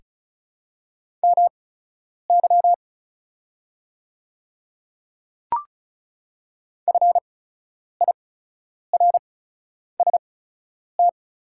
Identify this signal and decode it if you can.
Morse code, heard as a keyed tone, about 34 wpm, spaced as for 10 wpm, 690 Hz. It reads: MY FIRST
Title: 8x Character Spacing - Top 100 Words - 35wpm